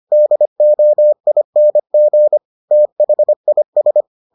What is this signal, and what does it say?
Morse code, heard as a keyed tone, about 25 wpm, 600 Hz. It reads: DOING THIS